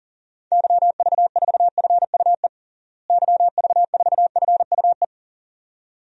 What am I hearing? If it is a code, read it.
Morse code, heard as a keyed tone, about 40 wpm, 700 Hz. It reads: YV4FUE YV4FUE
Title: DXCC-100 Call Signs - 6 Characters - Repeated 2x - 40wpm